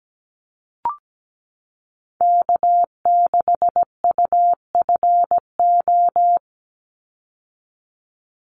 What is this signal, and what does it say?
Morse code, heard as a keyed tone, about 17 wpm, 700 Hz. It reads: K6UFO
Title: Call Signs - 17wpm